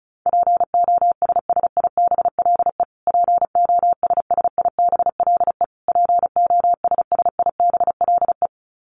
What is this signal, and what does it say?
Morse code, heard as a keyed tone, about 34 wpm, 700 Hz. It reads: POSSIBLE POSSIBLE POSSIBLE
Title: Top 500 Words Repeated 35wpm